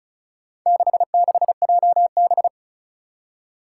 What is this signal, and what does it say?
Morse code, heard as a keyed tone, about 36 wpm, 700 Hz. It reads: 66JB
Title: Four Challenging Letter-Number Combinations 35wpm